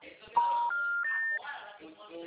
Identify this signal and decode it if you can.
Special information tone, with speech.